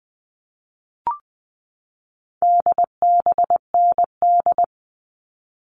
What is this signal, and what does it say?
Morse code, heard as a keyed tone, about 20 wpm, 700 Hz. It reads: DBND
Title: All Cut Numbers - 4-Digit 20wpm